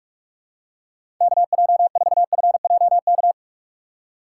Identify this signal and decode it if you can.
Morse code, heard as a keyed tone, about 46 wpm, 700 Hz. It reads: KJ4FJK
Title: USA Call Signs - 6x Character Spacing - 45wpm